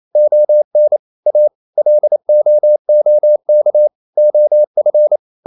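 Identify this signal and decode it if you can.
Morse code, heard as a keyed tone, about 28 wpm, 600 Hz. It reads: ON A LOOK OF
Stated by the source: Morse code practice at 28 wpm, text of Ken Follett's novel Never in English, from CW med Gnister.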